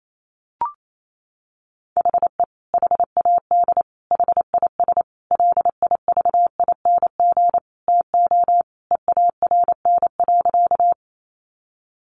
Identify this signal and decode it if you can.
Morse code, heard as a keyed tone, about 28 wpm, 700 Hz. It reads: HE HAD HIS LIVING TO EARN.
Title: Sentences from Top 1500 Words 28wpm